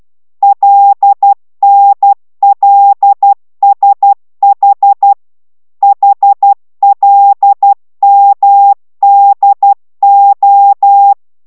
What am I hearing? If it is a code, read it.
Morse code, heard as a keyed tone, about 12 wpm, 800 Hz. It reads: LNLSH HLMDO
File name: taviro tanulas / cw 69 11 10